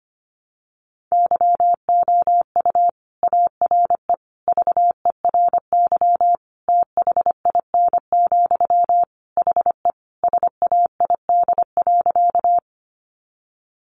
Morse code, heard as a keyed tone, about 25 wpm, 700 Hz. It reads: YOU ARE VERY THIN, HE SAID.
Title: Sentences from Top 1200 Words 25wpm